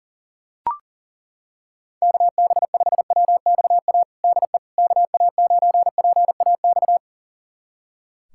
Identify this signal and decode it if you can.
Morse code, heard as a keyed tone, about 40 wpm, 700 Hz. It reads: KB5WXA DE KA9PAX
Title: USA Callsign DE Callsign - 6 Characters - 40wpm